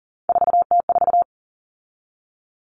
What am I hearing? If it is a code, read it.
Morse code, heard as a keyed tone, about 40 wpm, 700 Hz. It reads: 4T4
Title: Common Cut Numbers - 3-Digit 40wpm